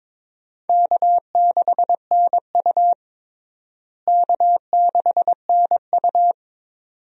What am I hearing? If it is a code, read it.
Morse code, heard as a keyed tone, about 22 wpm, 700 Hz. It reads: K6NU K6NU
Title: USA Call Signs - 4 Characters - Repeated 2x - 22wpm